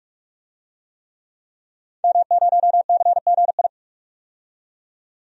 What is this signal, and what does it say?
Morse code, heard as a keyed tone, about 46 wpm, 700 Hz. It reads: M0CGI
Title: DXCC-100 Call Signs - 5 Characters - 45wpm